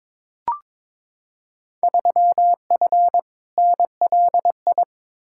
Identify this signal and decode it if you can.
Morse code, heard as a keyed tone, about 21 wpm, 700 Hz. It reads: 3F NLI
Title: Field Day Exchanges Encore - 22wpm